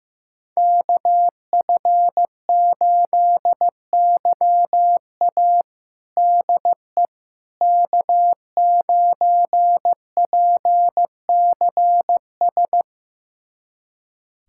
Morse code, heard as a keyed tone, about 15 wpm, 700 Hz. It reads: KF8YA DE K9PCS
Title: USA Callsign DE Callsign - 5 Characters - 15wpm